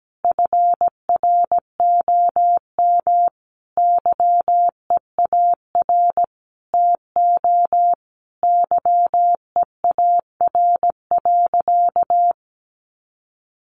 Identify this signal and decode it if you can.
Morse code, heard as a keyed tone, about 17 wpm, 700 Hz. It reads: FROM YEAR TO YEAR.
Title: Sentences from Top 1200 Words 17wpm